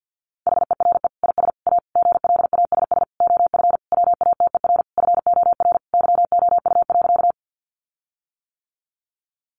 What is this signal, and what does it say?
Morse code, heard as a keyed tone, about 50 wpm, 700 Hz. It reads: HERE IS A GLASS OF WATER FOR YOU.
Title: Sentences from Top 1300 Words 50wpm